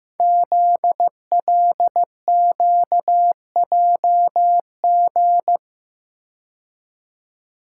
Morse code, heard as a keyed tone, about 15 wpm, 700 Hz. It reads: ZLQJG
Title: Code Groups - 5 Letters - 15wpm